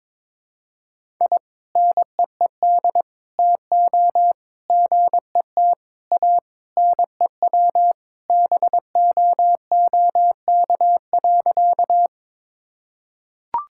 Morse code, heard as a keyed tone, about 22 wpm, 700 Hz. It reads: I NEED TO GET A NEW BOOK.
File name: practice / 22wpm / Sentences with 3-Word Phrases - 22wpm